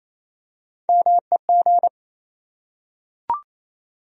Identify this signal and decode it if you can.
Morse code, heard as a keyed tone, about 27 wpm, 700 Hz. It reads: MEG
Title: USA Common Names - Max 5 Letters - 28wpm